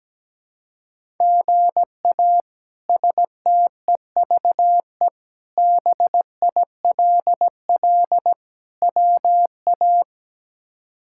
Morse code, heard as a keyed tone, about 17 wpm, 700 Hz. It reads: GA STEVE BILL WA
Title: K1USN SST QSO Practice - 17wpm